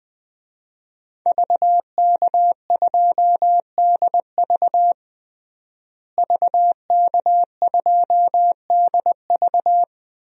Morse code, heard as a keyed tone, about 20 wpm, 700 Hz. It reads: VK2DV VK2DV
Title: DXCC-100 Call Signs - 5 Characters - Repeated 2x - 20wpm